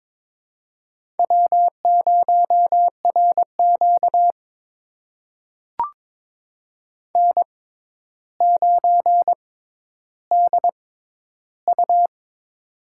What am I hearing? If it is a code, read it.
Morse code, heard as a keyed tone, about 22 wpm, spaced as for 11 wpm, 700 Hz. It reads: W0RQ N9DU
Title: USA Call Signs - 4 Characters - 6x Character Spacing - 22wpm